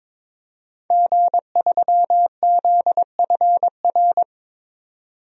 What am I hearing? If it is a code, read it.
Morse code, heard as a keyed tone, about 23 wpm, 700 Hz. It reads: G3ZFR
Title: DXCC-100 Call Signs - 5 Characters - 22wpm